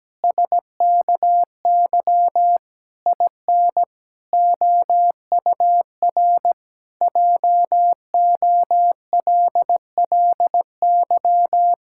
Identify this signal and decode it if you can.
Morse code, heard as a keyed tone, about 17 wpm, 700 Hz. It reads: SKY IN OUR JOLLY
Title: Pangram Challenge v2 - 17wpm